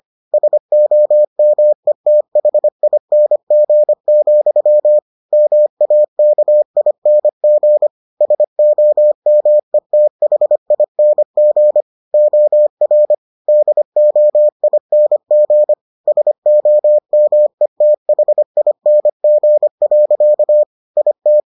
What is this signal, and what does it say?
Morse code, heard as a keyed tone, about 25 wpm, 600 Hz. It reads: SOMETHING, MAKING SOMETHING OR DOING SOMETHING. IT